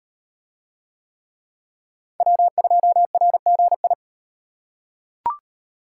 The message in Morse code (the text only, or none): W2RGI